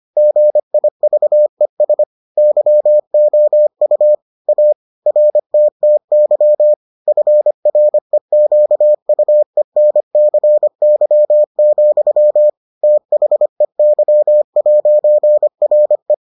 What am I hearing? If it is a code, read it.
Morse code, heard as a keyed tone, about 25 wpm, 600 Hz. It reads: GIVES YOU A RTTY FREQUENCY, THEY'RE